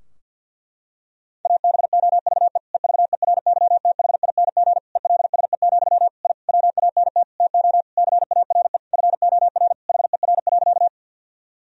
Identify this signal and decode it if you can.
Morse code, heard as a keyed tone, about 52 wpm, 700 Hz. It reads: ABOVE EVERYTHING ELSE, I WANT TO CARE FOR HER.
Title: Sentences from Top 1000 Words - Review Part 2 - 50wpm